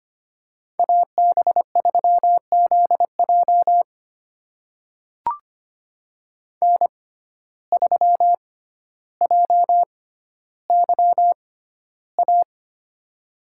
Morse code, heard as a keyed tone, about 25 wpm, spaced as for 13 wpm, 700 Hz. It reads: AB3ZJ N3JYA
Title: USA Call Signs - 5 Characters - 6x Character Spacing - 25wpm